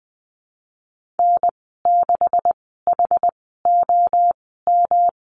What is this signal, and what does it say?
Morse code, heard as a keyed tone, about 20 wpm, spaced as for 15 wpm, 700 Hz. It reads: N6HOM
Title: 154 Lesson 25g - Letter p - Call Signs 20wpm